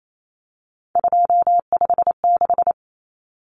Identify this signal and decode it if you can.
Morse code, heard as a keyed tone, about 28 wpm, 700 Hz. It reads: 256
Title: Common Cut Numbers - 3-Digit 28wpm